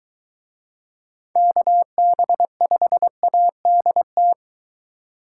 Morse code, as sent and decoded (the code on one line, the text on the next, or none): -.- -... ..... .- -.. -
KB5ADT